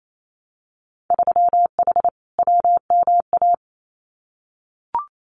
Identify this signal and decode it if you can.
Morse code, heard as a keyed tone, about 28 wpm, 700 Hz. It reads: 3H WMA
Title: Winter Field Day Exchanges Encore 28wpm